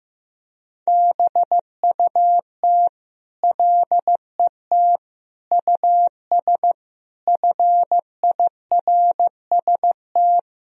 Morse code, heard as a keyed tone, about 15 wpm, 700 Hz. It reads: BUT LET US FIRST
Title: Sentences from Top 1000 Words - Review Part 4 - 15wpm